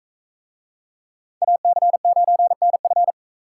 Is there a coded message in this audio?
Yes: Morse code from 1.4 to 3.1 s, heard as a keyed tone, about 44 wpm, 700 Hz. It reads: AC9NF